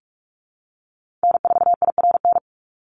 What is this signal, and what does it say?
Morse code, heard as a keyed tone, about 44 wpm, 700 Hz. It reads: N4IRN
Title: Call Signs Repeated 3x - 45wpm